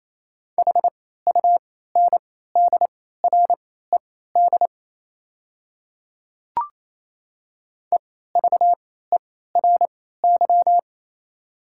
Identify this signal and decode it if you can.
Morse code, heard as a keyed tone, about 28 wpm, spaced as for 16 wpm, 700 Hz. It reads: HUNDRED EVERY
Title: 3x Character Spacing - Top 500 Words - 28wpm